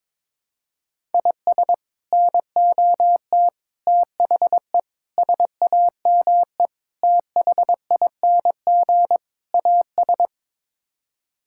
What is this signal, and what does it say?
Morse code, heard as a keyed tone, about 22 wpm, 700 Hz. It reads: IS NOT THE SAME THING AS
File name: practice / 22wpm / Sets of 6 Words - 22wpm